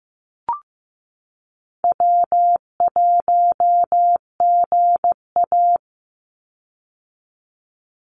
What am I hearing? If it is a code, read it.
Morse code, heard as a keyed tone, about 15 wpm, 700 Hz. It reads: W1GA
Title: USA Call Signs 15wpm